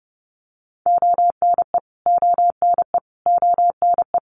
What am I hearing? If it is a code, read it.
Morse code, heard as a keyed tone, about 30 wpm, 700 Hz. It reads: ONE ONE ONE
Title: Top 200 Words Repeated 30wpm